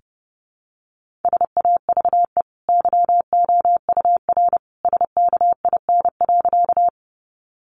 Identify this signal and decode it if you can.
Morse code, heard as a keyed tone, about 30 wpm, 700 Hz. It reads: SAVE YOUR SKIN.
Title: Sentences from Top 1300 Words 30wpm